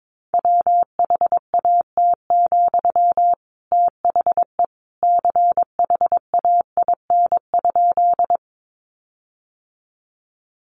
Morse code, heard as a keyed tone, about 22 wpm, 700 Hz. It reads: WHAT, THE CHAIN?
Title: Sentences from Top 1500 Words 22wpm